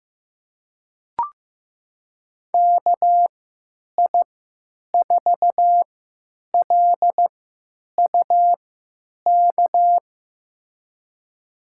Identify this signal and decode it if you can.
Morse code, heard as a keyed tone, about 15 wpm, spaced as for 9 wpm, 700 Hz. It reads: KI4LUK
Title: USA Call Signs - 3x Character Spacing - 15wpm